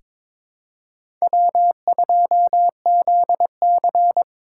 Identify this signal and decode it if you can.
Morse code, heard as a keyed tone, about 22 wpm, 700 Hz. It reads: W2ZC
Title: USA Call Signs - 3x Character Spacing - 22wpm